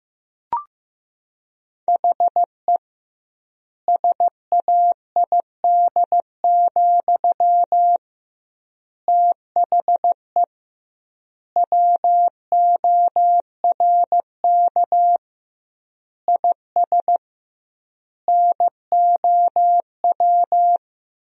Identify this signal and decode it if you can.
Morse code, heard as a keyed tone, about 15 wpm, 700 Hz. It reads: HE SAID, THE WORK IS NOW
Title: 2x Word-Spacing - Sentences from Top 400 Words - 15wpm